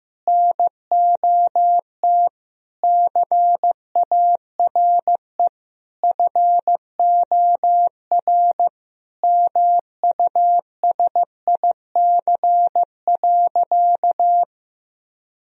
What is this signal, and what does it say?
Morse code, heard as a keyed tone, about 16 wpm, 700 Hz. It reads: NOT CARE FOR MUSIC.
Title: Sentences from Top 1000 Words - Review Part 1 - 15wpm